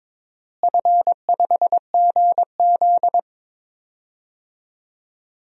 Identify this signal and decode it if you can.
Morse code, heard as a keyed tone, about 22 wpm, 700 Hz. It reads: F5GZ
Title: DXCC-100 Call Signs - 4 Characters - Repeated 3x - 22wpm